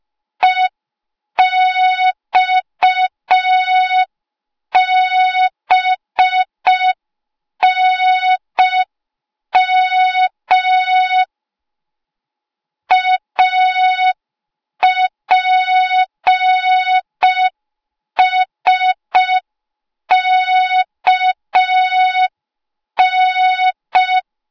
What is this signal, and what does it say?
Morse code, heard as a keyed tone, about 5 wpm, 3700 Hz. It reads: EXBNM APSKN